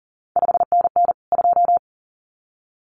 Morse code, heard as a keyed tone, about 39 wpm, 700 Hz. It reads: 5NN 2
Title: CQ WW DX Contest Exchange Encore - 40wpm